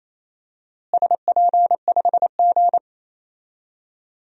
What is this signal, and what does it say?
Morse code, heard as a keyed tone, about 28 wpm, 700 Hz. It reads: SP5G
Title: DXCC-100 Call Signs - 4 Characters - Repeated 3x - 28wpm